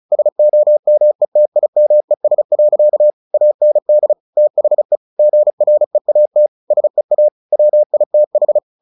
Morse code, heard as a keyed tone, about 34 wpm, 600 Hz. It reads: SOMETIMES. AND THE GREAT SEA WITH
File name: the old man and the sea 2 35wpm